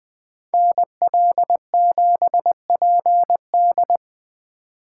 Morse code, heard as a keyed tone, about 20 wpm, 700 Hz. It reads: NL7PD